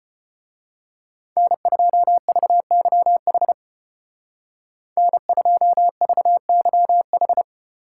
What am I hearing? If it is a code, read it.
Morse code, heard as a keyed tone, about 31 wpm, 700 Hz. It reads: N2VYH N2VYH
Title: Speed Racing - USA 1x3 Call Signs 25wpm